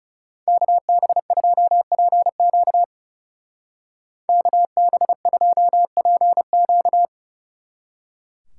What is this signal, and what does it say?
Morse code, heard as a keyed tone, about 40 wpm, 700 Hz. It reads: KB2PQ KB2PQ